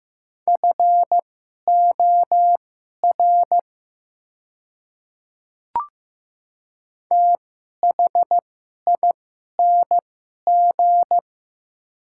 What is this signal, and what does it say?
Morse code, heard as a keyed tone, about 15 wpm, spaced as for 11 wpm, 700 Hz. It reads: FOR THING
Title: 2x Character Spacing - Top 100 Words - 15wpm